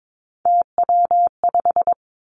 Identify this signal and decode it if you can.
Morse code, heard as a keyed tone, about 22 wpm, 700 Hz. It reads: TW5
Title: Three Challenging Letter-Number Combinations 22wpm